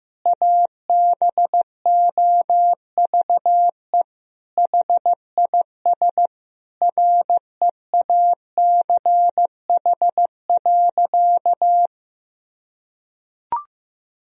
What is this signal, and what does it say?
Morse code, heard as a keyed tone, about 15 wpm, 700 Hz. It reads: ABOVE HIS REACH.
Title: Sentences from Top 1000 Words - Review Part 2 - 15wpm